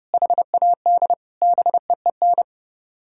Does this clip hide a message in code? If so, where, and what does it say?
Yes: Morse code from 0.1 to 2.4 s, heard as a keyed tone, about 38 wpm, 700 Hz. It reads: HAD BEEN